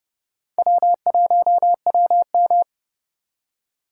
Morse code, heard as a keyed tone, about 28 wpm, 700 Hz. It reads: W1WM
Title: USA Call Signs - 4 Characters - 4x Character Spacing - 30wpm